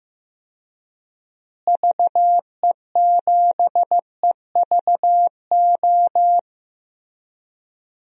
Morse code, heard as a keyed tone, about 15 wpm, 700 Hz. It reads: VE7EVO
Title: DXCC-100 Call Signs - 6 Characters - Repeated 2x - 15wpm